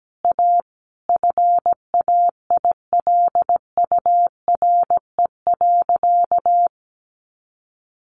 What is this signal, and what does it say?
Morse code, heard as a keyed tone, about 17 wpm, 700 Hz. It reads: A FAILURE.
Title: Sentences from Top 900 Words 17wpm